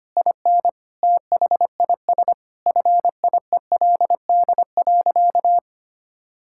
Morse code, heard as a keyed tone, about 25 wpm, 700 Hz. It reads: IN THIS FIELD.
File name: practice / 25wpm / Sentences with 4-Word Phrases - 25wpm